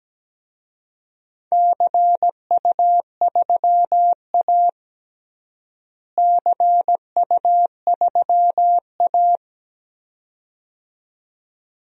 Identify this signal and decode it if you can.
Morse code, heard as a keyed tone, about 17 wpm, 700 Hz. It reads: CU3A CU3A
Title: DXCC-100 Call Signs - 4 Characters - Repeated 2x - 17wpm